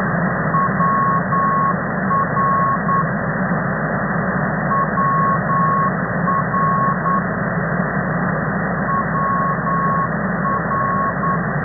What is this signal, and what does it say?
Morse code, heard as a keyed tone, about 9 wpm, 1100 Hz, with noise. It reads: WR WR WR
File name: as ndb